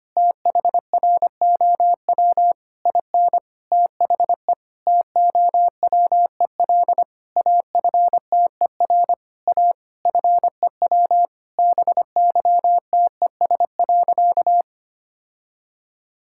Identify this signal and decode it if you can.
Morse code, heard as a keyed tone, about 25 wpm, 700 Hz. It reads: THROW IN THE TOWEL AFTER A FEW BYTES.
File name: practice / 25wpm / All-Age Appropriate Jokes - 25wpm